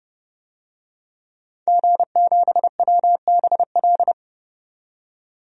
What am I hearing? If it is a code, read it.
Morse code, heard as a keyed tone, about 30 wpm, 700 Hz. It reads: G7WBL
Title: DXCC-100 Call Signs - 5 Characters - 30wpm